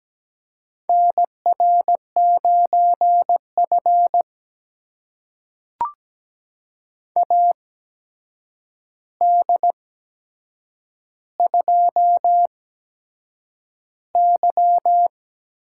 Morse code, heard as a keyed tone, about 17 wpm, spaced as for 7 wpm, 700 Hz. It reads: NR9F AD2Y
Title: USA Call Signs - 4 Characters - 8x Character Spacing - 17wpm